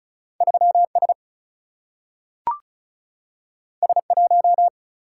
Morse code, heard as a keyed tone, about 35 wpm, 700 Hz. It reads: 3S S1
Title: Single-Letter Single-Number 35wpm